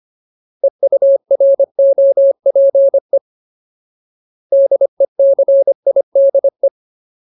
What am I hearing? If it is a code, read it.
Morse code, heard as a keyed tone, about 25 wpm, 550 Hz. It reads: EUROPE DECIDE
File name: Ultimate Word List / 25WPM Ultimate